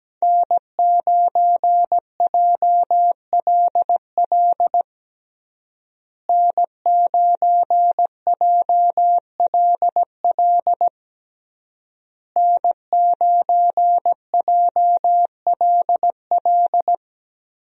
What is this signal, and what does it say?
Morse code, heard as a keyed tone, about 17 wpm, 700 Hz. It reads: N9JLL N9JLL N9JLL
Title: USA Call Signs - 5 Characters - Repeated 3x - 17wpm